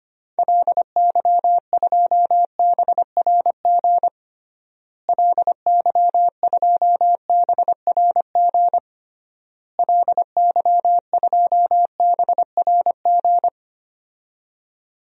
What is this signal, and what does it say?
Morse code, heard as a keyed tone, about 25 wpm, 700 Hz. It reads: LY2BRG LY2BRG LY2BRG
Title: DXCC-100 Call Signs - 6 Characters - Repeated 3x - 25wpm